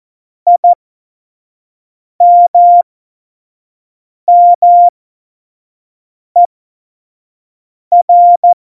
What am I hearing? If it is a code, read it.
Morse code, heard as a keyed tone, about 14 wpm, spaced as for 5 wpm, 700 Hz. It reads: IMMER
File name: Morsekurs / Witze / Witz 19. Dezember 2017, 25 F70